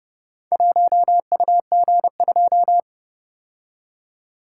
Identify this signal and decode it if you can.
Morse code, heard as a keyed tone, about 30 wpm, 700 Hz. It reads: 1UG2